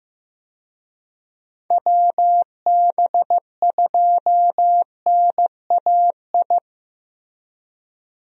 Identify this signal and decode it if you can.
Morse code, heard as a keyed tone, about 15 wpm, 700 Hz. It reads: WB2NAI
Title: USA Call Signs - 6 Characters - 15wpm